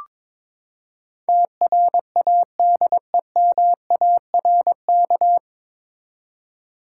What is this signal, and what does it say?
Morse code, heard as a keyed tone, about 22 wpm, 700 Hz. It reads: TRADEMARK
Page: Morse Code Ninja practice 22wpm